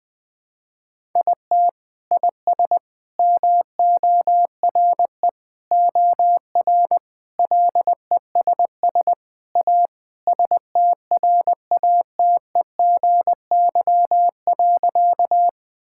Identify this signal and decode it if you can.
Morse code, heard as a keyed tone, about 20 wpm, 700 Hz. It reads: IT IS MORE OR LESS A STRATEGY.
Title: Sentences from Top 1000 Words - Review Part 5 - 20wpm